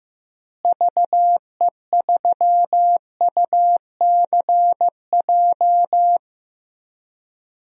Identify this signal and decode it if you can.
Morse code, heard as a keyed tone, about 15 wpm, 700 Hz. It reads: VE3UCJ
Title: DXCC-100 Call Signs - 6 Characters - Repeated 3x - 15wpm